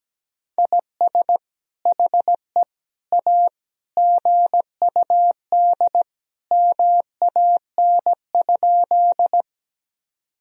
Morse code, heard as a keyed tone, about 17 wpm, 700 Hz. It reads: IS HE A GUD MAN?
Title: Sentences from Top 200 Words and Abbreviations 17wpm